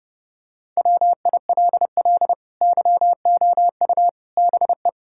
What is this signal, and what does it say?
Morse code, heard as a keyed tone, about 30 wpm, 700 Hz. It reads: WILL YOU BE
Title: Sentences from Top 1600 Words 30wpm